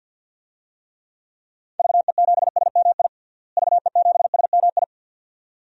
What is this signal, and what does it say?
Morse code, heard as a keyed tone, about 52 wpm, 700 Hz. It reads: VE7SMI VE7SMI
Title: DXCC-100 Call Signs - 6 Characters - Repeated 2x - 50wpm